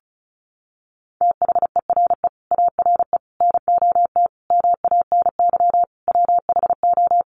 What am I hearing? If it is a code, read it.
Morse code, heard as a keyed tone, about 34 wpm, 700 Hz. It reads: THERE ARE NOT MANY WHO
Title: Sentences from Top 1300 Words 35wpm